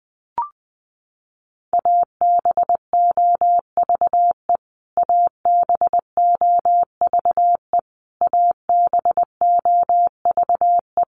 Morse code, heard as a keyed tone, about 20 wpm, 700 Hz. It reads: ABOVE ABOVE ABOVE
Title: Top 500 Words Repeated 20wpm